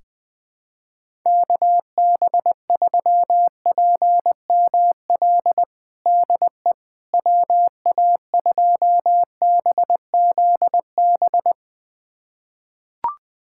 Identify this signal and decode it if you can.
Morse code, heard as a keyed tone, about 21 wpm, 700 Hz. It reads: KB3PML DE WA2BZB
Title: USA Callsign DE Callsign - 6 Characters - 20wpm